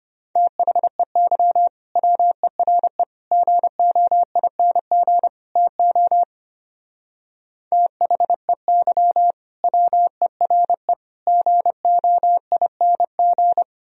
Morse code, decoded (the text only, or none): THEY WERE GOING TO THEY WERE GOING